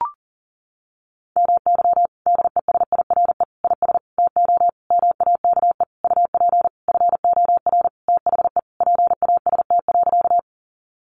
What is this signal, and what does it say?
Morse code, heard as a keyed tone, about 40 wpm, 700 Hz. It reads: MY DESIRE IS TO MAKE UP FOR THE PAST.
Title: Sentences from Top 1100 Words 40wpm